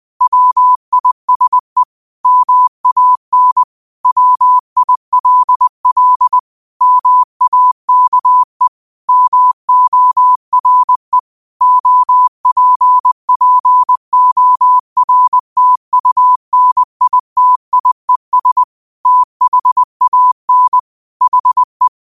Morse code, heard as a keyed tone, about 20 wpm, 1000 Hz. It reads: WISE MAN WILL MAKE MORE OPPORTUNITIES THAN HE